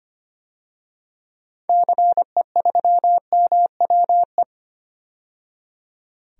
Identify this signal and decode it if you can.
Morse code, heard as a keyed tone, about 25 wpm, 700 Hz. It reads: CE3MWE
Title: DXCC-100 Call Signs - 6 Characters - 25wpm